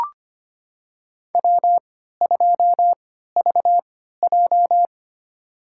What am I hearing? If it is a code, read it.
Morse code, heard as a keyed tone, about 25 wpm, spaced as for 14 wpm, 700 Hz. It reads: W2VJ
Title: USA Call Signs - 4 Characters - 3x Character Spacing - 25wpm